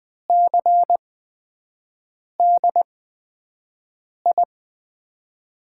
Morse code, heard as a keyed tone, about 20 wpm, spaced as for 5 wpm, 700 Hz. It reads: CDI
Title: USA Call Signs - 6 Characters - 8x Character Spacing - 20wpm